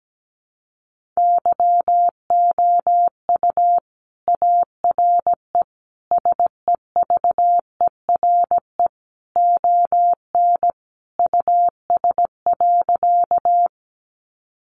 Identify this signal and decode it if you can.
Morse code, heard as a keyed tone, about 17 wpm, 700 Hz. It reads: YOU ARE SEVERE ON US.
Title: Sentences from Top 1400 Words 17wpm